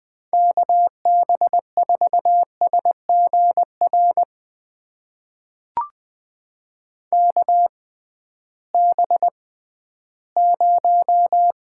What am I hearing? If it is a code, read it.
Morse code, heard as a keyed tone, about 21 wpm, spaced as for 13 wpm, 700 Hz. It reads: KB4SGR KB0
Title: USA Call Signs - 6 Characters - 6x Character Spacing - 20wpm